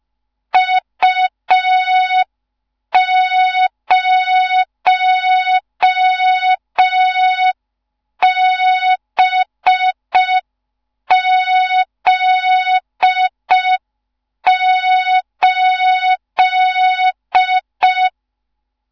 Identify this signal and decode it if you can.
Morse code, heard as a keyed tone, about 5 wpm, 3700 Hz. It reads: U0BZ8
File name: Morze-HA2MN / hang / 15 GYAKORLAT